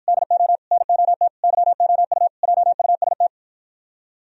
Morse code, heard as a keyed tone, about 55 wpm, 700 Hz. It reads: DO NOT YOU JUST